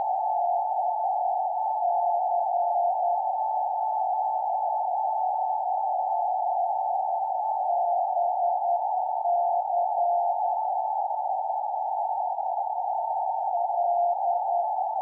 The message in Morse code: none